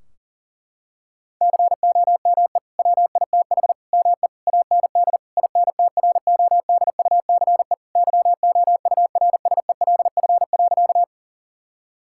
Morse code, heard as a keyed tone, about 37 wpm, 700 Hz. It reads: COME WITH ME AND INTRODUCE YOURSELF.
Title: Sentences from Top 1600 Words 40wpm